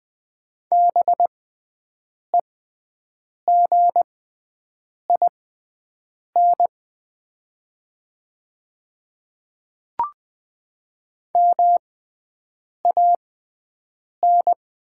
Morse code, heard as a keyed tone, about 20 wpm, spaced as for 7 wpm, 700 Hz. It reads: BEGIN MAN
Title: 6x Character Spacing - Top 300 Words - 20wpm